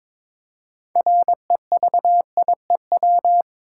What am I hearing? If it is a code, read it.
Morse code, heard as a keyed tone, about 22 wpm, 700 Hz. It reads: REVIEW
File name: practice / 22wpm / Easy Sentences with Hints - 22wpm